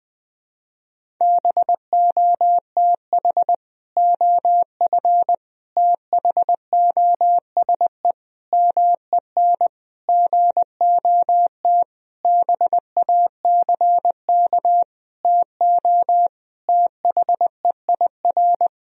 Morse code, heard as a keyed tone, about 21 wpm, 700 Hz. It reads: BOTH OF THOSE MEN GOT BACK TO THEIR
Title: Sentences from Top 1000 Words - Review Part 1 - 20wpm